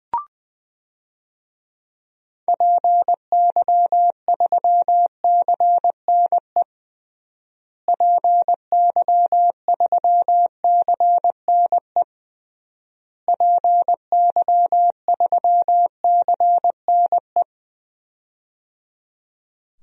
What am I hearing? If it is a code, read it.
Morse code, heard as a keyed tone, about 20 wpm, 700 Hz. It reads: PY3CNE PY3CNE PY3CNE
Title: DXCC-100 Call Signs - 6 Characters - Repeated 3x - 20wpm